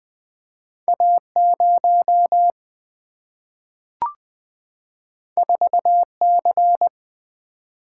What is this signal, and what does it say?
Morse code, heard as a keyed tone, about 20 wpm, 700 Hz. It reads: A 0 4 C
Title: Single-Letter Single-Number 20wpm